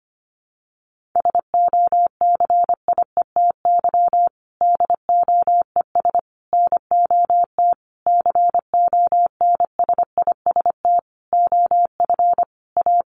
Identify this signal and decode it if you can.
Morse code, heard as a keyed tone, about 25 wpm, 700 Hz. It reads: SOCIETY DOES NOT CONSIST OF A
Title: Sentences from Top 1400 Words 25wpm